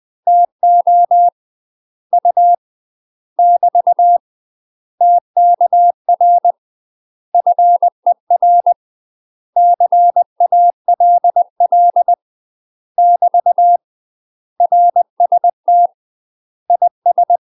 Morse code, heard as a keyed tone, about 20 wpm, 700 Hz. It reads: TO U <BT> TKR FER CALL <BT> RST IS
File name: UK MASTER QSOs TRAINING / 20WPM / UK MASTER QSOs 0058